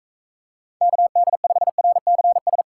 Morse code, heard as a keyed tone, about 42 wpm, 700 Hz. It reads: KD5RCS